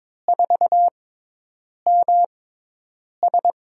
Morse code, heard as a keyed tone, about 21 wpm, spaced as for 8 wpm, 700 Hz. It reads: 4MS
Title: USA Call Signs - 5 Characters - 6x Character Spacing - 22wpm